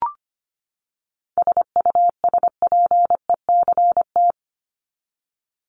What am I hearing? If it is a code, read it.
Morse code, heard as a keyed tone, about 25 wpm, 700 Hz. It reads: SUSPECT
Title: Top 1751-2000 Words 25wpm